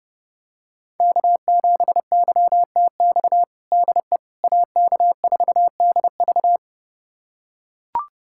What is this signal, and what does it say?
Morse code, heard as a keyed tone, about 32 wpm, 700 Hz. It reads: K7YTX DE AK4DV
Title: USA Callsign DE Callsign - 5 Characters - 30wpm